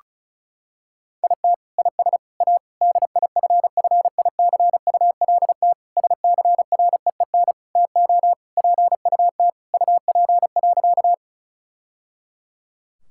Morse code, heard as a keyed tone, about 36 wpm, 700 Hz. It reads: IT IS A DIFFICULT SCREEN TO PUT UP.
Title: Sentences from Top 1000 Words - Review Part 6 - 35wpm